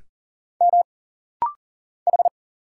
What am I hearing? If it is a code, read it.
Morse code, heard as a keyed tone, about 39 wpm, 700 Hz. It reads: M H
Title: Single Letter-Number Alt - Rapid-Fire 40wpm